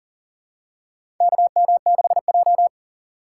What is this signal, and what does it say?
Morse code, heard as a keyed tone, about 40 wpm, 700 Hz. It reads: KM6J
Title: USA Call Signs - 4 Characters - 4x Character Spacing - 40wpm